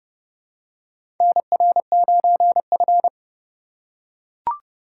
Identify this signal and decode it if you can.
Morse code, heard as a keyed tone, about 32 wpm, 700 Hz. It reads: NR9F
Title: USA Call Signs - 2x Character Spacing - 30wpm